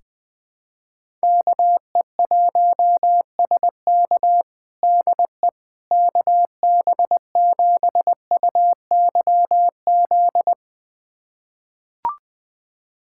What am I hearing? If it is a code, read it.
Morse code, heard as a keyed tone, about 20 wpm, 700 Hz. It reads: KE1SK DE KB7UYZ
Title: USA Callsign DE Callsign 20wpm